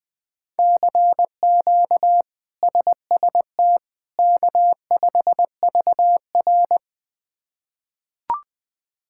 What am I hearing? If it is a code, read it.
Morse code, heard as a keyed tone, about 20 wpm, 700 Hz. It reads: CQ SST K5VR
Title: K1USN SST QSO Practice - 20wpm